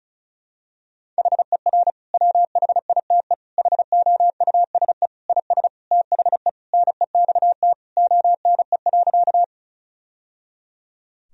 Morse code, heard as a keyed tone, about 36 wpm, 700 Hz. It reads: HER WHITE HOUSE IS THE NEXT ONE.
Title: Sentences from Top 1000 Words - Review Part 1 - 35wpm